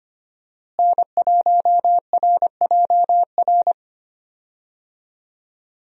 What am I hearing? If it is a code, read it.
Morse code, heard as a keyed tone, about 25 wpm, 700 Hz. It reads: N1RJR
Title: USA Call Signs - 5 Characters - Repeated 3x - 25wpm